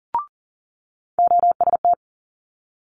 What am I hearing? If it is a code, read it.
Morse code, heard as a keyed tone, about 40 wpm, 700 Hz. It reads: OST
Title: Multi-letter Phonemes Quick Fire 40wpm